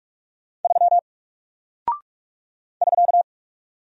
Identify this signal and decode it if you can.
Morse code, heard as a keyed tone, about 44 wpm, 700 Hz. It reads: 3 <SK>
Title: Single Character (Letter:Number:Punctuation:Prosign) Alt 45wpm